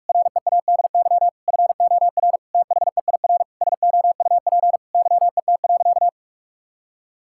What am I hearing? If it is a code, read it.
Morse code, heard as a keyed tone, about 42 wpm, 700 Hz. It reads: READY FOR THEIR SOUP YET.
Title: Sentences from Top 1000 Words - Review Part 6 - 45wpm